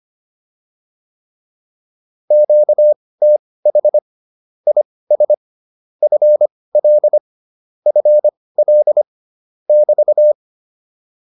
Morse code, heard as a keyed tone, about 25 wpm, spaced as for 18 wpm, 600 Hz. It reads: QTH IS FL FL <BT>
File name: QSO110 18